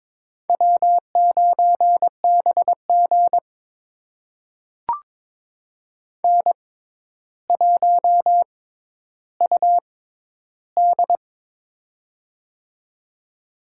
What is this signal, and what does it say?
Morse code, heard as a keyed tone, about 22 wpm, spaced as for 11 wpm, 700 Hz. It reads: W9BG N1UD
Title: USA Call Signs - 4 Characters - 6x Character Spacing - 22wpm